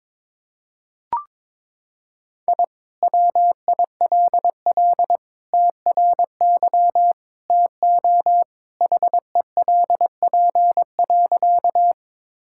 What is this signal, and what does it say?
Morse code, heard as a keyed tone, about 22 wpm, 700 Hz. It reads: I WILL TRY TO HELP.
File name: practice / 22wpm / Sentences with 3-Word Phrases - 22wpm